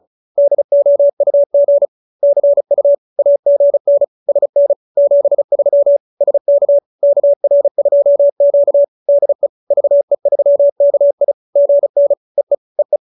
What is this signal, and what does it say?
Morse code, heard as a keyed tone, about 34 wpm, 570 Hz. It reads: DOUG CU AGN SN 73 SK KR2Q DE VE3KI GN EE EE